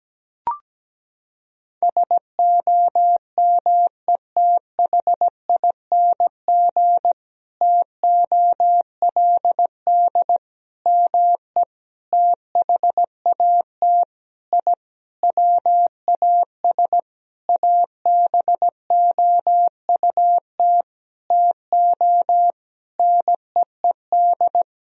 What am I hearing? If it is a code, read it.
Morse code, heard as a keyed tone, about 17 wpm, 700 Hz. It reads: SOMETHING TOLD ME THAT I WAS ABOUT TO NEED
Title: Sentences from Top 1000 Words - Review Part 1 - 17wpm